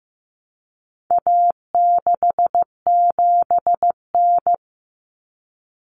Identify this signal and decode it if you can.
Morse code, heard as a keyed tone, about 15 wpm, 700 Hz. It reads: A67N
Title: Common Cut Numbers - 4-Digit 15wpm